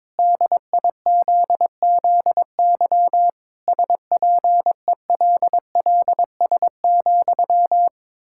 Morse code, heard as a keyed tone, about 22 wpm, 700 Hz. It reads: DIZZY SPELLS,
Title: Pangram Challenge v2 - 22wpm